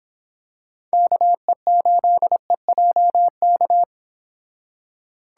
Morse code, heard as a keyed tone, about 26 wpm, 700 Hz. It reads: KE8EJK